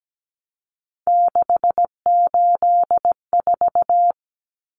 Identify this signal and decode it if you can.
Morse code, heard as a keyed tone, about 17 wpm, 700 Hz. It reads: 684
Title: Numbers - 3-Digit 17wpm